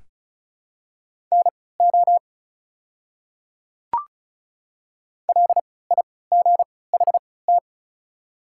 Morse code, heard as a keyed tone, about 36 wpm, spaced as for 20 wpm, 700 Hz. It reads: NO LIGHT
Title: 3x Character Spacing - Top 200 Words - 35wpm